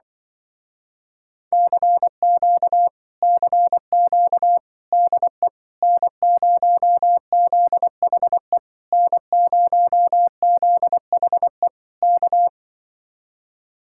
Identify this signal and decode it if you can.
Morse code, heard as a keyed tone, about 24 wpm, 700 Hz. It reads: CQ CQ DE N0ZHE N0ZHE K
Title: Speed-Racing Common QSO Phrases - 20wpm